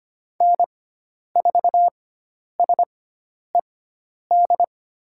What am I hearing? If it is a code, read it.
Morse code, heard as a keyed tone, about 25 wpm, spaced as for 10 wpm, 700 Hz. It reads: N4SED
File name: Lesson 16g - Letter c - Call Signs 25wpm